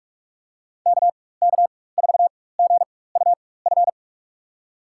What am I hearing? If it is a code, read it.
Morse code, heard as a keyed tone, about 46 wpm, spaced as for 21 wpm, 700 Hz. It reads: KK4GUF
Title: USA Call Signs - 6 Characters - 4x Character Spacing - 45wpm